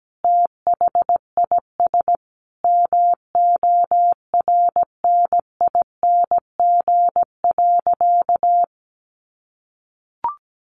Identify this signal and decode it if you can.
Morse code, heard as a keyed tone, about 17 wpm, 700 Hz. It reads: THIS MORNING.
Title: Sentences from Top 900 Words 17wpm